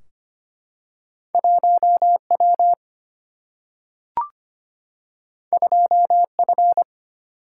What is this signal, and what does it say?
Morse code, heard as a keyed tone, about 25 wpm, 700 Hz. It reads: 1W 2F
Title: Single-Letter Single-Number 25wpm